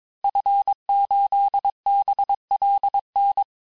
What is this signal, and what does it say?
Morse code, heard as a keyed tone, about 22 wpm, 800 Hz. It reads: F8BLN
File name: cours-du-mercredi / audios / cours-nr-860-FIN-de-QSOr-vitesse-21MPM